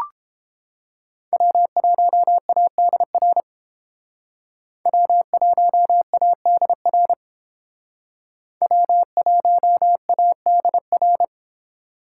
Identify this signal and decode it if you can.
Morse code, heard as a keyed tone, about 28 wpm, 700 Hz. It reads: W1ADR W1ADR W1ADR